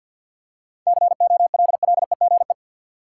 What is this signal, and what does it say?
Morse code, heard as a keyed tone, about 53 wpm, 700 Hz. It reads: COLLEGE
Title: Seven Letter Words - 50wpm